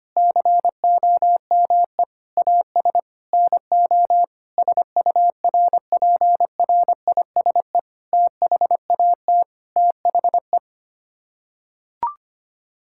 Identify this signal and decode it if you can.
Morse code, heard as a keyed tone, about 25 wpm, 700 Hz. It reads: COME AS NO SURPRISE THAT THE
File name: Sets of 9 Words - 25wpm